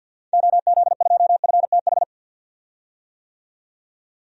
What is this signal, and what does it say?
Morse code, heard as a keyed tone, about 49 wpm, 700 Hz. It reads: OZ2FTH
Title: DXCC-100 Call Signs - 6 Characters - 50wpm